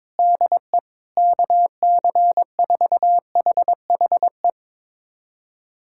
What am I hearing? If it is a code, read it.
Morse code, heard as a keyed tone, about 22 wpm, 700 Hz. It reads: DE KC4HHE
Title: USA Callsign DE Callsign 22wpm